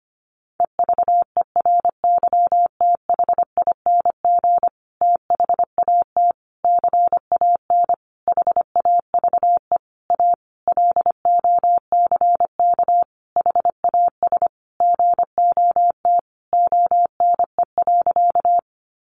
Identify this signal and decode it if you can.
Morse code, heard as a keyed tone, about 25 wpm, 700 Hz. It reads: EVERYTHING THAT CAN HAVE A LOCK HAS GOT ONE.
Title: Sentences from Top 1300 Words 25wpm